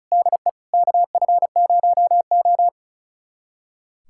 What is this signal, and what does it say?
Morse code, heard as a keyed tone, about 36 wpm, 700 Hz. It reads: DE KF0O
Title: USA Callsign DE Callsign - 4 Characters - 35wpm